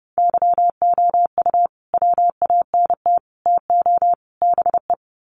Morse code, heard as a keyed tone, about 30 wpm, 700 Hz. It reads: YOU WANT TO BE
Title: Sets of 4 Words 30wpm